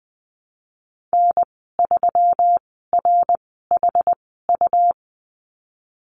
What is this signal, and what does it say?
Morse code, heard as a keyed tone, about 20 wpm, spaced as for 14 wpm, 700 Hz. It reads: N3RHU
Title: 147 Lesson 24g - Letter y - Call Signs 20wpm